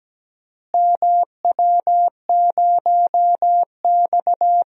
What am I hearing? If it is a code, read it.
Morse code, heard as a keyed tone, about 17 wpm, 700 Hz. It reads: MW0X
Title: DXCC-100 Call Signs - 4 Characters - Repeated 2x - 17wpm